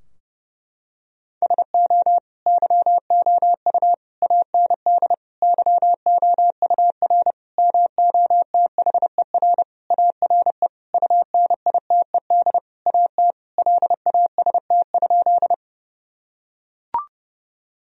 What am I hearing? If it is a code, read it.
Morse code, heard as a keyed tone, about 30 wpm, 700 Hz. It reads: SO YOU AND YOUR MOTHER ARE UNITED AT LAST?
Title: Sentences from Top 1000 Words - Review Part 4 - 30wpm